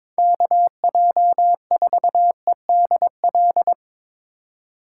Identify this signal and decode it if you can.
Morse code, heard as a keyed tone, about 24 wpm, 700 Hz. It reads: KJ4EDL
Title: Speed Racing - USA 2x3 Call Signs 22wpm